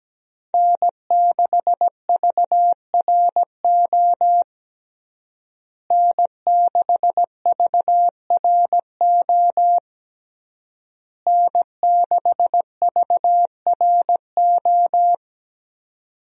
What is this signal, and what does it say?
Morse code, heard as a keyed tone, about 17 wpm, 700 Hz. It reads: N6VRO N6VRO N6VRO
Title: USA Call Signs - 5 Characters - Repeated 3x - 17wpm